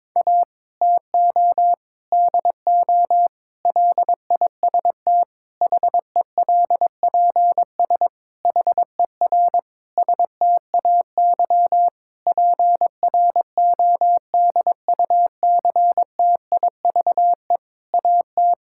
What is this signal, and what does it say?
Morse code, heard as a keyed tone, about 22 wpm, 700 Hz. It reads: A TO DO LIST HELPS HER STAY PRODUCTIVE AT
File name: Difficult Sentences with Hints - 22wpm